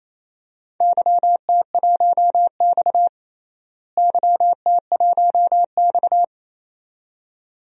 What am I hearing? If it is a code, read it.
Morse code, heard as a keyed tone, about 28 wpm, 700 Hz. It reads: YT1X YT1X
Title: DXCC-100 Call Signs - 4 Characters - Repeated 2x - 28wpm